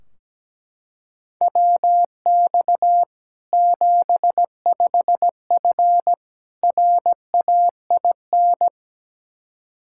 Morse code, heard as a keyed tone, about 17 wpm, 700 Hz. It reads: WX 75F RAIN
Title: Practice QSO - Exchange 2 - Info - 17wpm